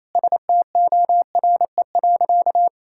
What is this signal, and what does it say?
Morse code, heard as a keyed tone, about 28 wpm, 700 Hz. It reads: STORE.
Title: Sentences from Top 1000 Words - Review Part 5 - 28wpm